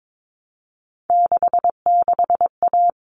Morse code, heard as a keyed tone, about 22 wpm, 700 Hz. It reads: 66A
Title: All Cut Numbers - 3-Digit 22wpm